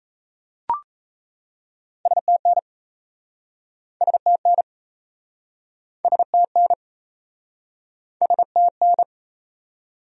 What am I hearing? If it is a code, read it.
Morse code, heard as a keyed tone, about 38 wpm, 700 Hz. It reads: STN STN STN STN